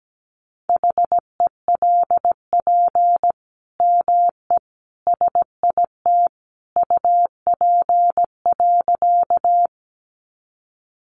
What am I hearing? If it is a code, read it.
Morse code, heard as a keyed tone, about 17 wpm, 700 Hz. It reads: HELP ME SIT UP.
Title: Sentences from Top 500 Words 17wpm